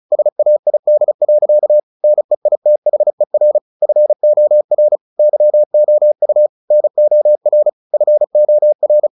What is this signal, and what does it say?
Morse code, heard as a keyed tone, about 34 wpm, 600 Hz. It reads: SAID. NEITHER FOR YOU NOR FOR
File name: the old man and the sea 2 35wpm